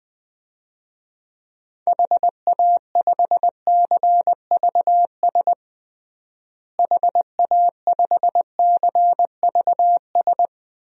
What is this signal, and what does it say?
Morse code, heard as a keyed tone, about 21 wpm, 700 Hz. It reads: HA5CVS HA5CVS
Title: DXCC-100 Call Signs - 6 Characters - Repeated 2x - 20wpm